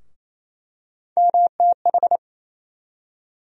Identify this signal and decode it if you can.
Morse code, heard as a keyed tone, about 28 wpm, 700 Hz. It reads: MTH
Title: USA Call Sign Postfixes - 3 Letters 28wpm